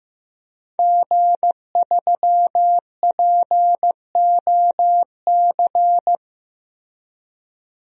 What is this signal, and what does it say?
Morse code, heard as a keyed tone, about 15 wpm, 700 Hz. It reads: G3POC
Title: DXCC-100 Call Signs - 5 Characters - Repeated 3x - 15wpm